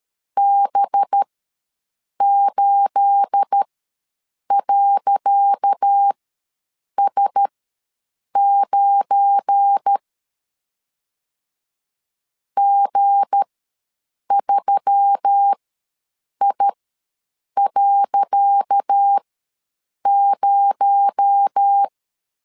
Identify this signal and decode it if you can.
Morse code, heard as a keyed tone, about 13 wpm, spaced as for 7 wpm, 800 Hz. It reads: B8.S9 G3I.0
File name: LEZIONI / 10 SIGNS / 10C DOT ws 12wpm 3548